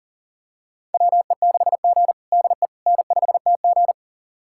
Morse code, heard as a keyed tone, about 40 wpm, 700 Hz. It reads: WE6G DE N5TG
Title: USA Callsign DE Callsign - 4 Characters - 40wpm